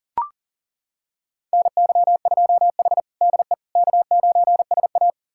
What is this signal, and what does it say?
Morse code, heard as a keyed tone, about 40 wpm, 700 Hz. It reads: NY2H DE K9SA
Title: USA Callsign DE Callsign - 4 Characters - 40wpm